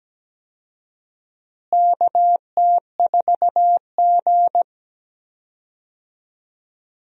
Morse code, heard as a keyed tone, about 17 wpm, 700 Hz. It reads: KT4G
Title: USA Call Signs - 4 Characters - Repeated 3x - 17wpm